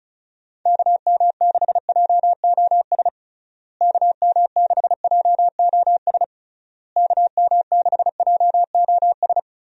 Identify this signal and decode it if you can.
Morse code, heard as a keyed tone, about 34 wpm, 700 Hz. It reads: KM6JOS KM6JOS KM6JOS
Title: USA Call Signs - 6 Characters - Repeated 3x - 35wpm